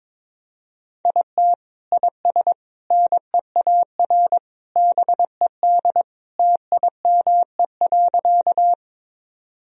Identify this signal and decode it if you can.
Morse code, heard as a keyed tone, about 22 wpm, 700 Hz. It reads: IT IS NEAR BED TIME.
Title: Sentences from Top 1000 Words - Review Part 1 - 22wpm